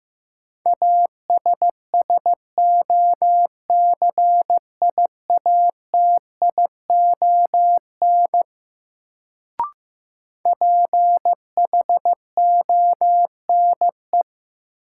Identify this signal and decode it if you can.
Morse code, heard as a keyed tone, about 15 wpm, 700 Hz. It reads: ASSOCIATION PHONE
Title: Sentences from Top 1100 Words - New Word Drill - 15wpm